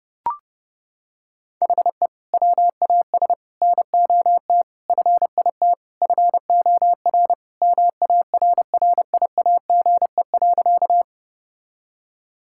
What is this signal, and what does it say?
Morse code, heard as a keyed tone, about 32 wpm, 700 Hz. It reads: HE WAS NOT FIT FOR MARRIAGE.
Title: Sentences from Top 1000 Words - Review Part 5 - 30wpm